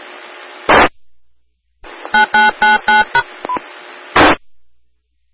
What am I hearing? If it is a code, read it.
DTMF, heard as telephone touch tones, tones of about 150 ms. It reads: 9999#
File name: RR Skywarn / W4AP Repeater